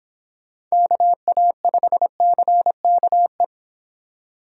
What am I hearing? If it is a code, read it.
Morse code, heard as a keyed tone, about 26 wpm, 700 Hz. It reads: KA5CKE